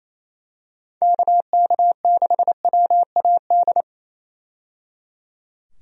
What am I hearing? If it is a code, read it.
Morse code, heard as a keyed tone, about 28 wpm, 700 Hz. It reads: KK6WAD
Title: USA Call Signs - 6 Characters - 28wpm